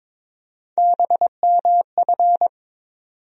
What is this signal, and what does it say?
Morse code, heard as a keyed tone, about 22 wpm, 700 Hz. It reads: BMF
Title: USA Call Sign Postfixes - 3 Letters 22wpm